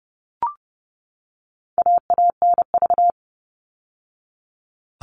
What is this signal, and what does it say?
Morse code, heard as a keyed tone, about 30 wpm, 700 Hz. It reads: AANV
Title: All Cut Numbers - 4-Digit 30wpm